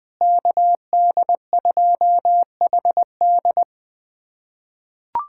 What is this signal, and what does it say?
Morse code, heard as a keyed tone, about 20 wpm, 700 Hz. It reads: KD2HD
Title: USA Callsign DE Callsign - 5 Characters - 20wpm